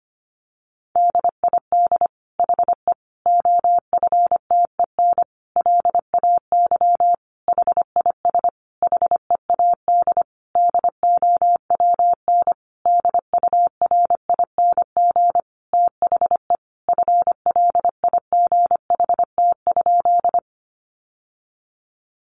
Morse code, heard as a keyed tone, about 25 wpm, 700 Hz. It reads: DID HE OFTEN LAY HIS HEAD DOWN DURING THE FLIGHT?
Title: Sentences from Top 1000 Words 25wpm